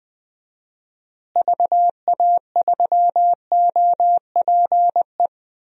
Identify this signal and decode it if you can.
Morse code, heard as a keyed tone, about 20 wpm, 700 Hz. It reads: VA3OPE
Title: DXCC-100 Call Signs - 6 Characters - Repeated 2x - 20wpm